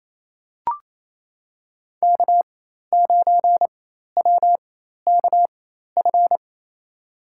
Morse code, heard as a keyed tone, about 27 wpm, spaced as for 13 wpm, 700 Hz. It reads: K9WKF
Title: USA Call Signs - 5 Characters - 4x Character Spacing - 28wpm